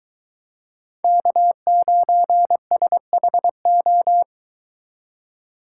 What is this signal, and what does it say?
Morse code, heard as a keyed tone, about 23 wpm, 700 Hz. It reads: K9SHO